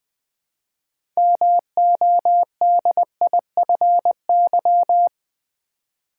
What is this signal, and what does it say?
Morse code, heard as a keyed tone, about 20 wpm, 700 Hz. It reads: MODIFY
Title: Build A Word - 2 Letters At A Time - 6-Letter Words - 20wpm